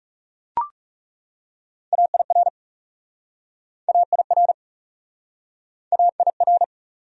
Morse code, heard as a keyed tone, about 36 wpm, 700 Hz. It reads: AIR AIR AIR
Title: Speed-Racing Top 200 Words 30wpm